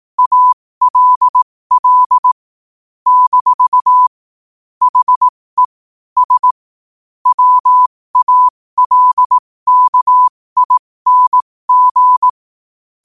Morse code, heard as a keyed tone, about 18 wpm, spaced as for 15 wpm, 1000 Hz. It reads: ALL - HE S WALKING
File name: qotdfiles / 15wpm